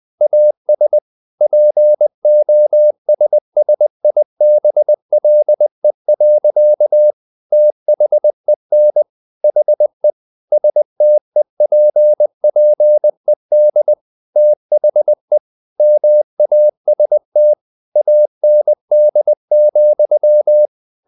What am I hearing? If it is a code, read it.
Morse code, heard as a keyed tone, about 20 wpm, 600 Hz. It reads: AS POSSIBLE. THEN HE STEPPED THE MAST AND,
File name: the old man and the sea part 2 20wpm